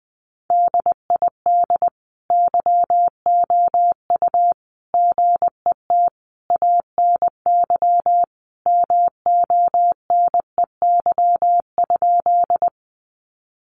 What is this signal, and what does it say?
Morse code, heard as a keyed tone, about 20 wpm, 700 Hz. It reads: DID YOU GET ANY MONEY?
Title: Sentences from Top 400 Words 20wpm